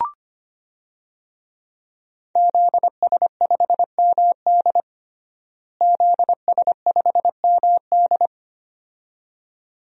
Morse code, heard as a keyed tone, about 25 wpm, 700 Hz. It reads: ZS5MD ZS5MD
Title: DXCC-100 Call Signs - 5 Characters - Repeated 2x - 25wpm